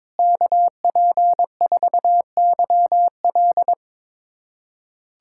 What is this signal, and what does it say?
Morse code, heard as a keyed tone, about 22 wpm, 700 Hz. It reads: KP4YL
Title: USA Callsign DE Callsign - 5 Characters - 22wpm